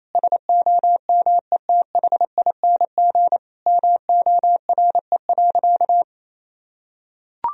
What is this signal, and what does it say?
Morse code, heard as a keyed tone, about 28 wpm, 700 Hz. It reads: SOMETHING MORE.
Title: Sentences from Top 1000 Words - Review Part 1 - 28wpm